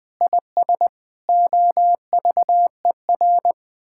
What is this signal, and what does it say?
Morse code, heard as a keyed tone, about 19 wpm, 700 Hz. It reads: IS OVER
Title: Sentences with 8-Word Phrases - 20wpm